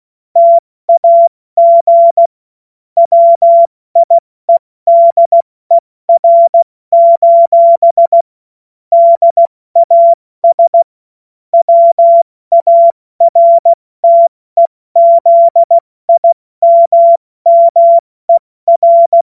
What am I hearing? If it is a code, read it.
Morse code, heard as a keyed tone, about 16 wpm, 680 Hz. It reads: TAG WIEDER: DAS WARTEZIMMER
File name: Morsekurs / Witze / Witz 12 März 2024 70 F80